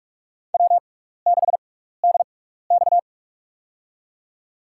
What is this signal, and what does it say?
Morse code, heard as a keyed tone, about 43 wpm, spaced as for 15 wpm, 700 Hz. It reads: W6DX
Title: USA Call Signs - 4 Characters - 6x Character Spacing - 45wpm